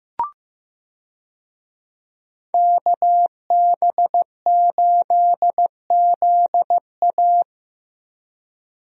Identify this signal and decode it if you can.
Morse code, heard as a keyed tone, about 15 wpm, 700 Hz. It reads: KB8ZA
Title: USA Call Signs - 5 Characters - Repeated 3x - 15wpm